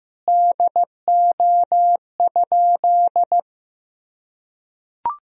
Morse code, heard as a keyed tone, about 15 wpm, 700 Hz. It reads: DO?
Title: Sentences from Top 100 Words and Abbreviations 15wpm